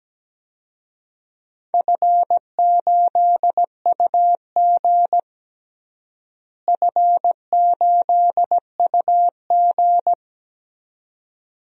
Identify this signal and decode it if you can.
Morse code, heard as a keyed tone, about 17 wpm, 700 Hz. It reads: F8UG F8UG
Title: DXCC-100 Call Signs - 4 Characters - Repeated 2x - 17wpm